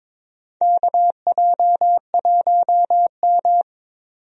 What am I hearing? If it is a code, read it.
Morse code, heard as a keyed tone, about 22 wpm, 700 Hz. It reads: KJ1M
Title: USA Call Signs - 4 Characters - 22wpm